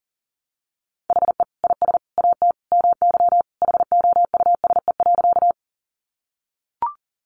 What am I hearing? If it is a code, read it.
Morse code, heard as a keyed tone, about 40 wpm, 700 Hz. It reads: HE IS AT MY HOUSE.
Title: Sentences from Top 200 Words 40wpm